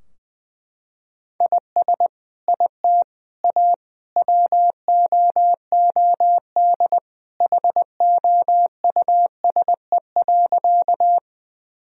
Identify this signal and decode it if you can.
Morse code, heard as a keyed tone, about 21 wpm, 700 Hz. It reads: IS IT A WOOD HOUSE.
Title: Sentences from Top 1000 Words - Review Part 2 - 20wpm